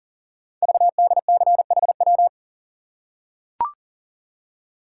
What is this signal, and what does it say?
Morse code, heard as a keyed tone, about 42 wpm, 690 Hz. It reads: VDCHW
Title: Code Groups - 5 Letters - 40wpm